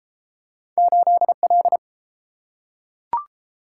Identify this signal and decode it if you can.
Morse code, heard as a keyed tone, about 29 wpm, 700 Hz. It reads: 8L